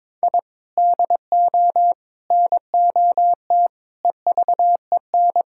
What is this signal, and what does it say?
Morse code, heard as a keyed tone, about 22 wpm, 700 Hz. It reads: I DO NOT EVEN
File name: practice / 22wpm / Sentences with 4-Word Phrases - 22wpm